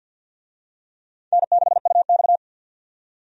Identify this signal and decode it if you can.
Morse code, heard as a keyed tone, about 46 wpm, 700 Hz. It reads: N6UX
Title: USA Call Signs - 4 Characters - 50wpm